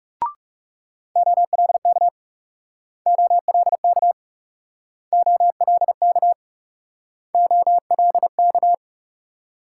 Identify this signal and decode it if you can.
Morse code, heard as a keyed tone, about 40 wpm, 700 Hz. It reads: OLK OLK OLK OLK